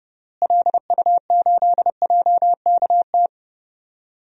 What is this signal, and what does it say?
Morse code, heard as a keyed tone, about 32 wpm, 700 Hz. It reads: LU8JKT
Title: DXCC-100 Call Signs - 6 Characters - Repeated 3x - 30wpm